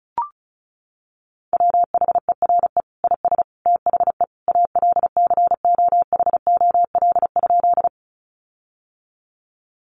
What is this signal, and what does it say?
Morse code, heard as a keyed tone, about 34 wpm, 700 Hz. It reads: WHERE IS THE ALCOHOL?
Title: Sentences from Top 1400 Words 35wpm